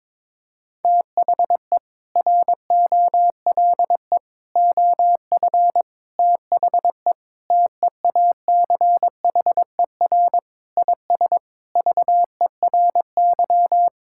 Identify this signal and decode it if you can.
Morse code, heard as a keyed tone, about 22 wpm, 700 Hz. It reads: THE ROLE OF THE TEACHER IS VERY
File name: practice / 22wpm / Sentences with 4-Word Phrases - 22wpm